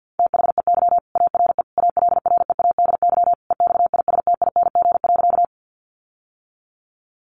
Morse code, heard as a keyed tone, about 50 wpm, 700 Hz. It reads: THEY ARE ALREADY EXISTING.
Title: Sentences from Top 1500 Words 50wpm